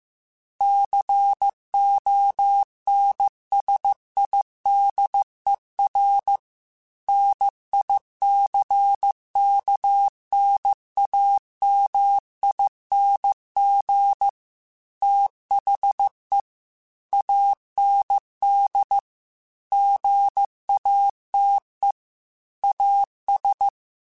Morse code, heard as a keyed tone, about 15 wpm, 800 Hz. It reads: CONSIDER NICKNAMING THE AND GATE AS
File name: Technical Subjects / Technical Subjects-mp3s / AND Gate @13